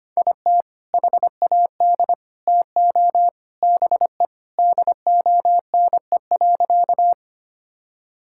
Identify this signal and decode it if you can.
Morse code, heard as a keyed tone, about 25 wpm, 700 Hz. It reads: IT HAD TO BE DONE.
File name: practice / 25wpm / Sentences with 3-Word Phrases - 25wpm